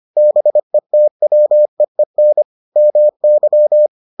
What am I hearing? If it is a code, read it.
Morse code, heard as a keyed tone, about 25 wpm, 600 Hz. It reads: BETWEEN MY